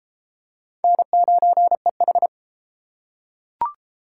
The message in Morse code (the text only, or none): N9EH